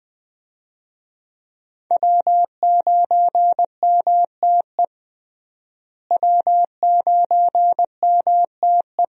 Morse code, heard as a keyed tone, about 20 wpm, 700 Hz. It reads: W9MTE W9MTE
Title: USA Call Signs - 5 Characters - Repeated 2x - 20wpm